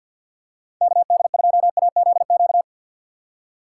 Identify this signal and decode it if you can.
Morse code, heard as a keyed tone, about 52 wpm, 700 Hz. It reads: KD2AZQ
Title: USA Call Signs - 6 Characters - 3x Character Spacing - 50wpm